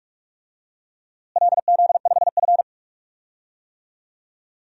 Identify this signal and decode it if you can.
Morse code, heard as a keyed tone, about 43 wpm, 700 Hz. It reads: LZ5F
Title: DXCC-100 Call Signs - 4 Characters - Repeated 3x - 45wpm